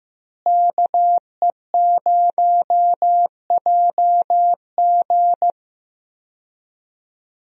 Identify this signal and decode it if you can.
Morse code, heard as a keyed tone, about 16 wpm, 700 Hz. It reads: KE0JG
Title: USA Call Signs - 5 Characters - Repeated 3x - 15wpm